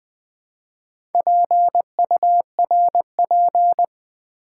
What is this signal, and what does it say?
Morse code, heard as a keyed tone, about 20 wpm, 700 Hz. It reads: PURP
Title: Build A Word - 2 Letters At A Time - 6-Letter Words - 20wpm